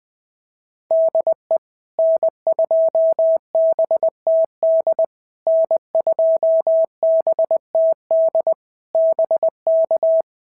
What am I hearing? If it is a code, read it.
Morse code, heard as a keyed tone, about 19 wpm, 650 Hz. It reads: DE N2BTD N2BTD BK
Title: POTA QSO Practice - 20wpm